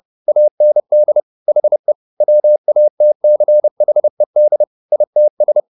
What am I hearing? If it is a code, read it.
Morse code, heard as a keyed tone, about 30 wpm, 600 Hz. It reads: AND HE WATCHED ITS